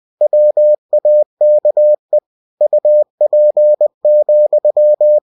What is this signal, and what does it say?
Morse code, heard as a keyed tone, about 20 wpm, 600 Hz. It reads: WAKE UP,
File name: cw-books / winnie-the-pooh / winnie-20wpm-page-0037